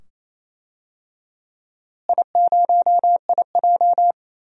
Morse code, heard as a keyed tone, about 28 wpm, 700 Hz. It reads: I0IJ
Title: DXCC-100 Call Signs - 4 Characters - Repeated 2x - 28wpm